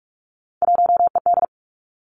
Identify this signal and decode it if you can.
Morse code, heard as a keyed tone, about 46 wpm, 700 Hz. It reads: 1ED
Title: Three Challenging Letter-Number Combinations 45wpm